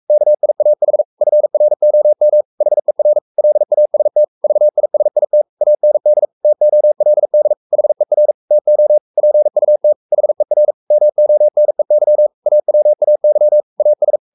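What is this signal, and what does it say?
Morse code, heard as a keyed tone, about 44 wpm, 600 Hz. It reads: KIAH FROM HER LAST VISIT AND TOLD HER TO PUT HER MONEY AWAY AS
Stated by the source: Morse code at 43 wpm sending Ken Follett's novel Never in English, from CW med Gnister.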